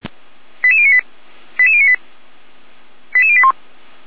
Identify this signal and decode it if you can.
Five-tone selective call (ZVEI), heard as quick runs of five tones, tones of about 70 ms. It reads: 88098 88098 88081